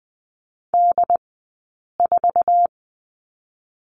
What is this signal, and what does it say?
Morse code, heard as a keyed tone, about 20 wpm, 700 Hz. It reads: D 4
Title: 066 Lesson 12d - Letter d - Two Characters 20wpm